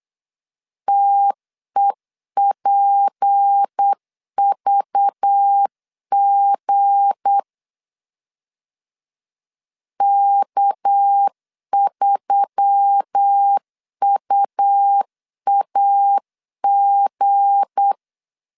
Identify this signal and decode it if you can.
Morse code, heard as a keyed tone, about 8 wpm, 800 Hz. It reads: TEPVG K3UAG
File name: LEZIONI / 7 GF9Z / 7 G 8wpm 3118